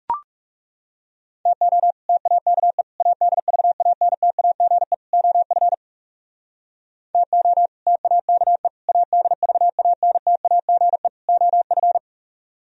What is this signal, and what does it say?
Morse code, heard as a keyed tone, about 37 wpm, 700 Hz. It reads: TO TAKE ADVANTAGE OF TO TAKE ADVANTAGE OF
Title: Speed-Racing - Sets of 4 Words 30wpm